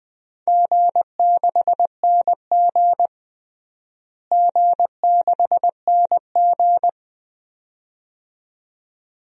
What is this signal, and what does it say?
Morse code, heard as a keyed tone, about 20 wpm, 700 Hz. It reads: G6NG G6NG
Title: DXCC-100 Call Signs - 4 Characters - Repeated 2x - 20wpm